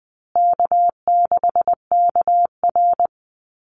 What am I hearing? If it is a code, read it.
Morse code, heard as a keyed tone, about 20 wpm, 700 Hz. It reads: K6KR
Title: Easy Call Signs Repeated 2x - 20wpm